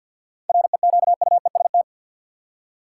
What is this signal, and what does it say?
Morse code, heard as a keyed tone, about 47 wpm, 700 Hz. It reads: REQUEST
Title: Seven Letter Words - 50wpm